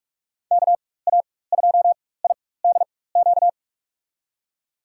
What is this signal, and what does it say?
Morse code, heard as a keyed tone, about 46 wpm, spaced as for 21 wpm, 700 Hz. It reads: KA2IDQ
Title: USA Call Signs - 6 Characters - 4x Character Spacing - 45wpm